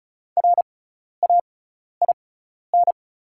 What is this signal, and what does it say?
Morse code, heard as a keyed tone, about 36 wpm, spaced as for 12 wpm, 700 Hz. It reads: RAIN
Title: 6x Character Spacing - Top 300 Words - 35wpm